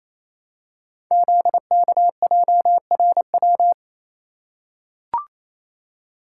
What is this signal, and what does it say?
Morse code, heard as a keyed tone, about 28 wpm, 700 Hz. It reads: ZKJRW
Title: Code Groups - 5 Letter:Numbers - 28wpm